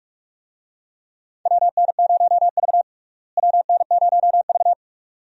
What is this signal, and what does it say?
Morse code, heard as a keyed tone, about 42 wpm, 700 Hz. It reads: WN0V WN0V
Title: USA Call Signs - 4 Characters - Repeated 2x - 45wpm